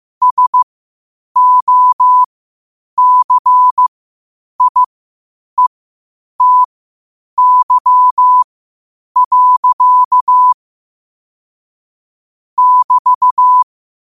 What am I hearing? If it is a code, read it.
Morse code, heard as a keyed tone, about 15 wpm, spaced as for 8 wpm, 1000 Hz. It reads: SOCIETY. <BT>
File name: qotdfiles / 8wpm